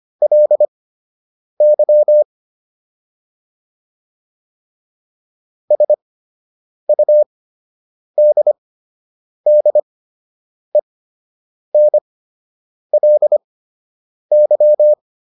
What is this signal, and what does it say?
Morse code, heard as a keyed tone, about 25 wpm, spaced as for 8 wpm, 600 Hz. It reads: LY SUDDENLY